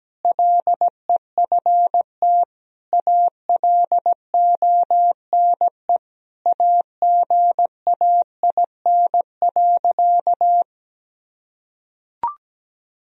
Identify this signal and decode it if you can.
Morse code, heard as a keyed tone, about 17 wpm, 700 Hz. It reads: LEFT ALONE AGAIN.
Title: Sentences from Top 1000 Words - Review Part 1 - 17wpm